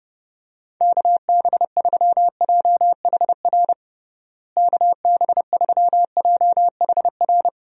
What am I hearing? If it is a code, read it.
Morse code, heard as a keyed tone, about 30 wpm, 700 Hz. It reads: KB3JHR KB3JHR
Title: USA Call Signs - 6 Characters - Repeated 2x - 30wpm